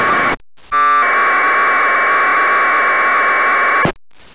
Packet radio (AX.25, as in APRS).